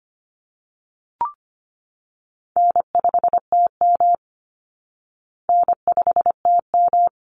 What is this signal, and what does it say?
Morse code, heard as a keyed tone, about 25 wpm, 700 Hz. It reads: N5TM N5TM
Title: Call Signs Repeated 3x - 25wpm